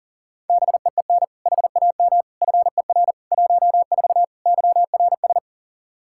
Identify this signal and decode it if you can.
Morse code, heard as a keyed tone, about 40 wpm, 700 Hz. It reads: BEEN HAM FER 14 YRS
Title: Practice QSO - Exchange 3 - More Info - 40wpm